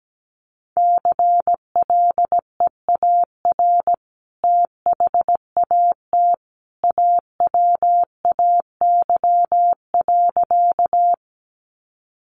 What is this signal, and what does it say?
Morse code, heard as a keyed tone, about 17 wpm, 700 Hz. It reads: CLEAR THAT AWAY.
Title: Sentences from Top 600 Words 17wpm